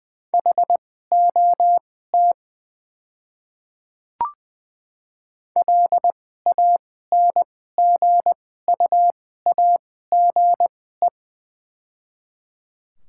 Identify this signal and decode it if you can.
Morse code, heard as a keyed tone, about 21 wpm, spaced as for 15 wpm, 700 Hz. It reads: HOT LANGUAGE
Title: 2x Character Spacing - Top 500 Words - 20wpm